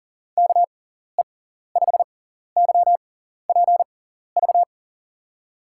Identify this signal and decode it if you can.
Morse code, heard as a keyed tone, about 43 wpm, spaced as for 14 wpm, 700 Hz. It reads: KE5YPV
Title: USA Call Signs - 6 Characters - 6x Character Spacing - 40wpm